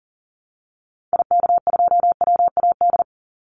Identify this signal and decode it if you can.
Morse code, heard as a keyed tone, about 39 wpm, 700 Hz. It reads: IK2WAD
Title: Call Signs Repeated 2x - 40wpm